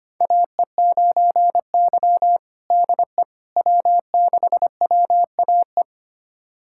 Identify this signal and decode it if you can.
Morse code, heard as a keyed tone, about 25 wpm, 700 Hz. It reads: AE9Y DE W6WAE
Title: USA Callsign DE Callsign 25wpm